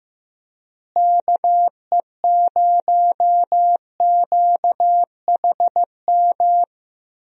Morse code, heard as a keyed tone, about 16 wpm, 700 Hz. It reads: KE0QHM
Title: USA Call Signs - 6 Characters - Repeated 2x - 15wpm